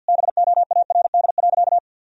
Morse code, heard as a keyed tone, about 50 wpm, 700 Hz. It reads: BOARD.
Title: Sentences from Top 1000 Words - Review Part 4 - 50wpm